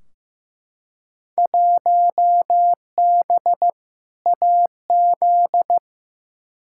Morse code, heard as a keyed tone, about 15 wpm, 700 Hz. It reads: 1B AZ
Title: Field Day Exchanges - 15wpm